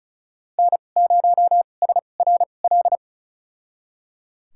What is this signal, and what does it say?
Morse code, heard as a keyed tone, about 36 wpm, spaced as for 25 wpm, 700 Hz. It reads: N0SRL
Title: USA Call Signs - 5 Characters - 2x Character Spacing - 35wpm